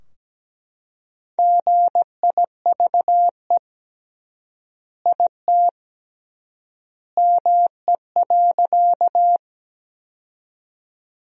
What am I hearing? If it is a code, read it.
Morse code, heard as a keyed tone, about 17 wpm, 700 Hz. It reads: GIVE IT ME.
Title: 3x Word-Spacing - Sentences from Top 200 Words - 17wpm